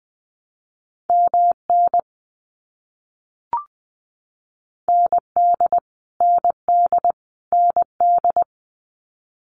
Morse code, heard as a keyed tone, about 20 wpm, 700 Hz. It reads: MN ND ND ND
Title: US States Repeated 3x 20wpm